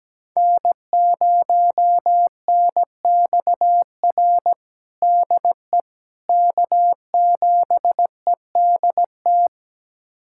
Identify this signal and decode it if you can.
Morse code, heard as a keyed tone, about 17 wpm, 700 Hz. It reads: N0NXR DE K7EDT
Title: USA Callsign DE Callsign - 5 Characters - 17wpm